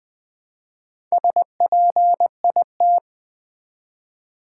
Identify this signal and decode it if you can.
Morse code, heard as a keyed tone, about 20 wpm, 700 Hz. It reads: SPIT
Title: Build A Word - 1 Letter At A Time - 3 to 5-Letter Words - 20wpm